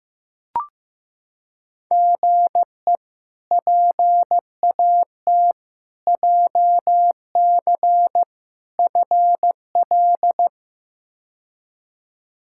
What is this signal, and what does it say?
Morse code, heard as a keyed tone, about 15 wpm, 700 Hz. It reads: GE PAT JC FL
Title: K1USN SST QSO Practice - 15wpm